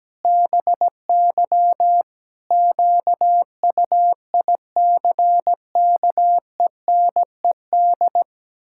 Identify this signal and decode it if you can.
Morse code, heard as a keyed tone, about 17 wpm, 700 Hz. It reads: BY QUICKENED
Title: Pangram Challenge v2 - 17wpm